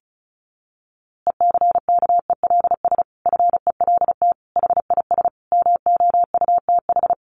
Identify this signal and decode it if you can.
Morse code, heard as a keyed tone, about 34 wpm, 700 Hz. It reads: ECKELS FELT HIS MOUTH
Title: A Sound Of Thunder 35wpm